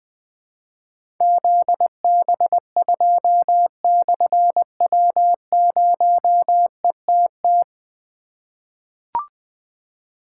Morse code, heard as a keyed tone, about 20 wpm, 700 Hz. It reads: ZB2/W0ETT
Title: Difficult Call Signs 20wpm